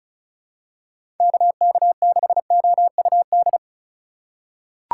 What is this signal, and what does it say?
Morse code, heard as a keyed tone, about 36 wpm, 700 Hz. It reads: KK6OUD
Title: USA Call Signs - 6 Characters - 2x Character Spacing - 35wpm